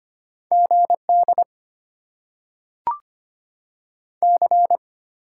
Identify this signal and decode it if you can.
Morse code, heard as a keyed tone, about 25 wpm, 700 Hz. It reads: GD C
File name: Top 90 QSO Elements - 8x Character Spacing - 25wpm